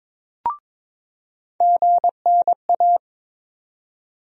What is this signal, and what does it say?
Morse code, heard as a keyed tone, about 22 wpm, 700 Hz. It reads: GNA
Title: USA Call Sign Postfixes - 3 Letters 22wpm